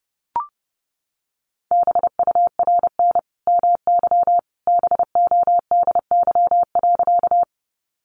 Morse code, heard as a keyed tone, about 30 wpm, 700 Hz. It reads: BURN MY BODY.
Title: Sentences from Top 1300 Words 30wpm